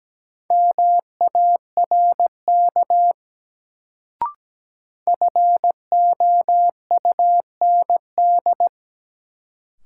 Morse code, heard as a keyed tone, about 17 wpm, 700 Hz. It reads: MARK FOUND
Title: Sentences from Top 300 Words - New Word Drill - 17wpm